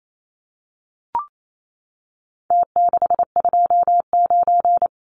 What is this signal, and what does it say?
Morse code, heard as a keyed tone, about 28 wpm, 700 Hz. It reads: T629
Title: Numbers - Contest Style - 4-Digit 28wpm